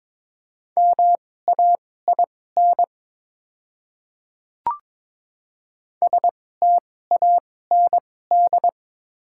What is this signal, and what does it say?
Morse code, heard as a keyed tone, about 22 wpm, spaced as for 16 wpm, 700 Hz. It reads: MAIN STAND
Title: 2x Character Spacing - Top 500 Words - 22wpm